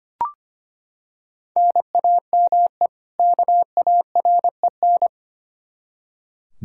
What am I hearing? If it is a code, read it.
Morse code, heard as a keyed tone, about 25 wpm, 700 Hz. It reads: NAME KAREN
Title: Practice QSO - Exchange 1 - Beginning - 25wpm